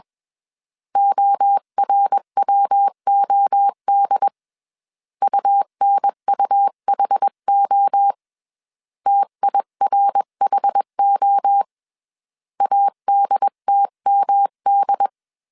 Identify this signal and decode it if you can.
Morse code, heard as a keyed tone, about 21 wpm, 800 Hz. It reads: ORWOD UNUHO TIRHO ADTMD
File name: CW / DET / 1 3 1500C S B 20wpm 1754